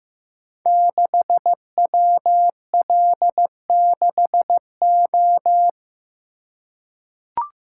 Morse code, heard as a keyed tone, about 15 wpm, 700 Hz. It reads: 6WL6O
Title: Code Groups - 5 Letter:Numbers - 15wpm